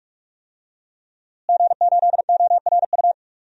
Morse code, heard as a keyed tone, about 41 wpm, 700 Hz. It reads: G8ORU
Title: DXCC-100 Call Signs - 5 Characters - Repeated 2x - 45wpm